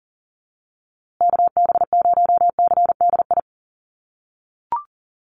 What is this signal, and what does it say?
Morse code, heard as a keyed tone, about 40 wpm, 700 Hz. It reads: KB0CDI
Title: USA Call Signs Repeated 2x 40wpm